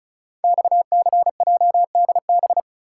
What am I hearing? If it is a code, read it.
Morse code, heard as a keyed tone, about 36 wpm, 700 Hz. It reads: XCJDB
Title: Code Groups - 5 Letters - 35wpm